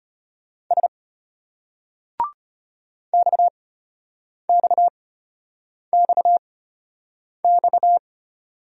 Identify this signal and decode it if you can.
Morse code, heard as a keyed tone, about 33 wpm, spaced as for 8 wpm, 700 Hz. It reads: S XXXX